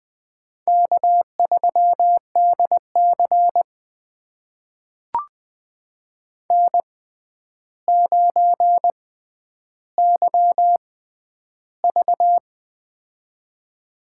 Morse code, heard as a keyed tone, about 20 wpm, spaced as for 10 wpm, 700 Hz. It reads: K3DC N9YV
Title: USA Call Signs - 4 Characters - 6x Character Spacing - 20wpm